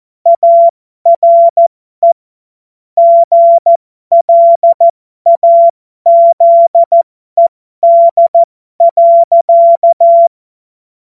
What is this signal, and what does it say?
Morse code, heard as a keyed tone, about 14 wpm, spaced as for 12 wpm, 680 Hz. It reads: ARE GLAZED.
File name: Morsekurs / Witze